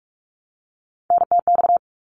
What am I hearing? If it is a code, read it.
Morse code, heard as a keyed tone, about 44 wpm, 700 Hz. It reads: NTX
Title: ARRL Sections 45wpm